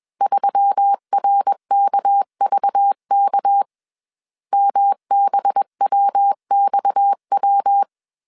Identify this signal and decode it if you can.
Morse code, heard as a keyed tone, about 21 wpm, 800 Hz. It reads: 3RKVK MBWXW
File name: CW / DET / 1 6 1500C S A 20wpm 2048